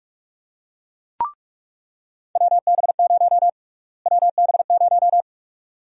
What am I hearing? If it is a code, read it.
Morse code, heard as a keyed tone, about 46 wpm, 700 Hz. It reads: WB0 WB0
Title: USA Call Sign Prefixes - Repeated 2x - 45wpm